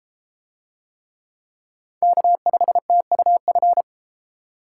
Morse code, heard as a keyed tone, about 30 wpm, 700 Hz. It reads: K5TUF